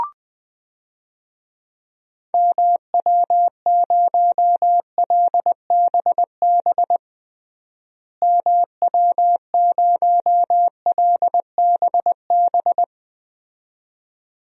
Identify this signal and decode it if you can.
Morse code, heard as a keyed tone, about 21 wpm, 700 Hz. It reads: MW0LBB MW0LBB
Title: DXCC-100 Call Signs - 6 Characters - Repeated 2x - 20wpm